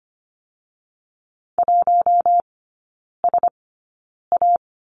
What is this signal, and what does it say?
Morse code, heard as a keyed tone, about 25 wpm, spaced as for 9 wpm, 700 Hz. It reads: 1SA
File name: Lesson 15e - Number 5 - Three Characters 25wpm